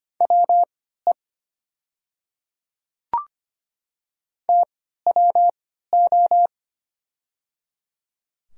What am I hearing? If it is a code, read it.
Morse code, heard as a keyed tone, about 25 wpm, spaced as for 14 wpm, 700 Hz. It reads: WE TWO
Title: 3x Character Spacing - Top 200 Words - 25wpm